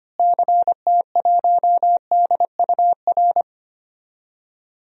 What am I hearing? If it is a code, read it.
Morse code, heard as a keyed tone, about 25 wpm, 700 Hz. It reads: CT1DUR
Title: DXCC-100 Call Signs - 6 Characters - Repeated 3x - 25wpm